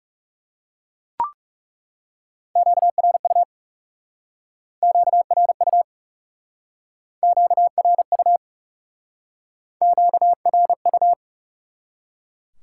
Morse code, heard as a keyed tone, about 38 wpm, 700 Hz. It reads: QRU QRU QRU QRU